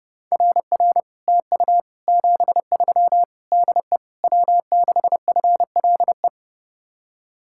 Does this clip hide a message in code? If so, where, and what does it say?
Yes: Morse code from 0.3 to 6.3 s, heard as a keyed tone, about 32 wpm, 700 Hz. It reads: RR TU 73 DE W6FLE